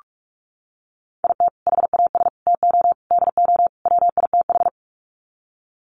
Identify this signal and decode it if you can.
Morse code, heard as a keyed tone, about 44 wpm, 700 Hz. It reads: IT HAS TO DO WITH
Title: Sets of 5 Words 45wpm